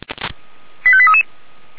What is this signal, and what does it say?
Five-tone selective call (ZVEI), heard as quick runs of five tones, tones of about 70 ms. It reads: 85833